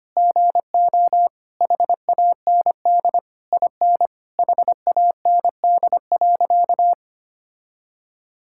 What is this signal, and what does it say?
Morse code, heard as a keyed tone, about 25 wpm, 700 Hz. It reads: GO HAND IN HAND.
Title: Sentences from Top 1000 Words - Review Part 5 - 25wpm